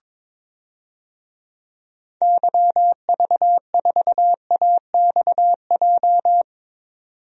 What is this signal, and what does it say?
Morse code, heard as a keyed tone, about 22 wpm, 700 Hz. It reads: YV4AXJ
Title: DXCC-100 Call Signs - 6 Characters - 22wpm